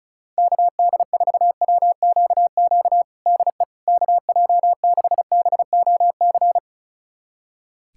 Morse code, heard as a keyed tone, about 34 wpm, 690 Hz. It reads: KD4WQQ DE KJ6BOC
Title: USA Callsign DE Callsign - 6 Characters - 35wpm